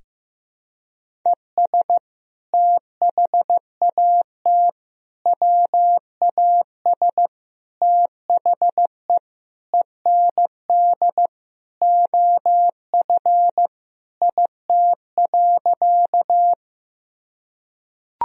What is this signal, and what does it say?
Morse code, heard as a keyed tone, about 15 wpm, 700 Hz. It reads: ES THAT WAS THE END OF IT.
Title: Sentences from Top 200 Words and Abbreviations 15wpm